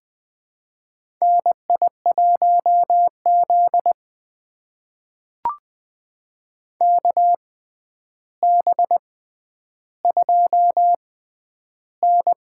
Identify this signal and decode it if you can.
Morse code, heard as a keyed tone, about 21 wpm, spaced as for 10 wpm, 700 Hz. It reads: NI1Z KB2N
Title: USA Call Signs - 4 Characters - 6x Character Spacing - 20wpm